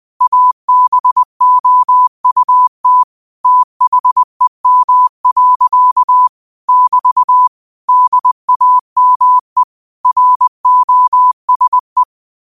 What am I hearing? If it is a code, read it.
Morse code, heard as a keyed tone, about 20 wpm, 1000 Hz. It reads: ABOUT THEM. <BT> DAME ROSE
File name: qotdfiles / 20wpm